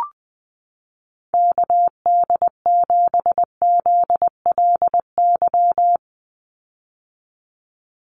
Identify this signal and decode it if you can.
Morse code, heard as a keyed tone, about 20 wpm, 700 Hz. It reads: KD7ZLY
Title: USA Call Signs 20wpm